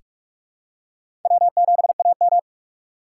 Morse code, heard as a keyed tone, about 46 wpm, 700 Hz. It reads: W7AM
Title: USA Call Signs - 3x Character Spacing - 45wpm